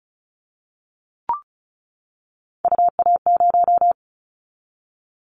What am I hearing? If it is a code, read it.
Morse code, heard as a keyed tone, about 34 wpm, 700 Hz. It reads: UA0
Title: Callsign Prefixes for Top 100 DXCC - 35wpm